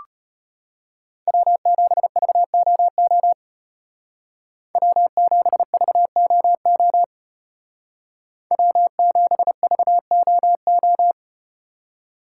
Morse code, heard as a keyed tone, about 34 wpm, 700 Hz. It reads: W7VOO W7VOO W7VOO